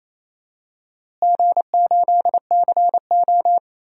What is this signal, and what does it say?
Morse code, heard as a keyed tone, about 28 wpm, 700 Hz. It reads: G8CO
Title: DXCC-100 Call Signs - 4 Characters - Repeated 2x - 28wpm